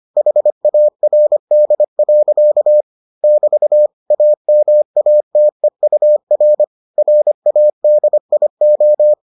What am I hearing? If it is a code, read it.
Morse code, heard as a keyed tone, about 25 wpm, 600 Hz. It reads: HARD. <BT> AMATEUR RADIO